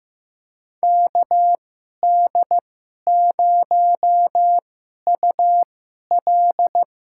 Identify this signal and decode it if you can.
Morse code, heard as a keyed tone, about 15 wpm, spaced as for 11 wpm, 700 Hz. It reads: KD0UL
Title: USA Call Signs - 6 Characters - 2x Character Spacing - 15wpm